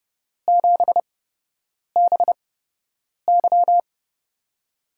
Morse code, heard as a keyed tone, about 31 wpm, spaced as for 8 wpm, 700 Hz. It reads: 7BY
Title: USA Call Signs - 5 Characters - 8x Character Spacing - 30wpm